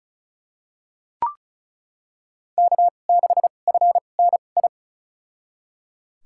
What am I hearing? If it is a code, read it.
Morse code, heard as a keyed tone, about 36 wpm, spaced as for 25 wpm, 690 Hz. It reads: K6FNI
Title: USA Call Signs - 5 Characters - 2x Character Spacing - 35wpm